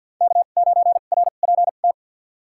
Morse code, heard as a keyed tone, about 47 wpm, spaced as for 35 wpm, 690 Hz. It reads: K9RPT
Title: USA Call Signs - 5 Characters - 2x Character Spacing - 50wpm